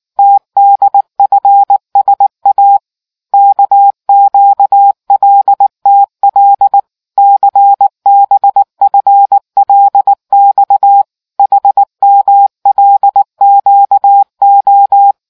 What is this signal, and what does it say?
Morse code, heard as a keyed tone, about 19 wpm, 800 Hz. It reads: TDFSA KQLTL CBFLX HMLQO